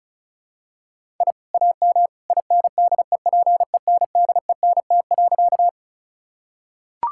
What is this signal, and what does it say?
Morse code, heard as a keyed tone, about 34 wpm, 700 Hz. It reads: I AM INDEPENDENT.
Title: Sentences from Top 1000 Words - Review Part 6 - 35wpm